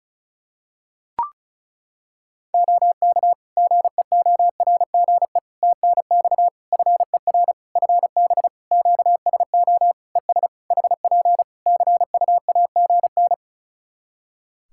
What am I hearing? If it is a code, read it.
Morse code, heard as a keyed tone, about 34 wpm, 700 Hz. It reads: OK GEORGE TNX FER FB QSO ES HP CUAGN
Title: K1USN SST QSO Practice 35wpm